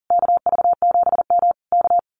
Morse code, heard as a keyed tone, about 41 wpm, 700 Hz. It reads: KV7M K
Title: Common QSO Phrases Encore 40wpm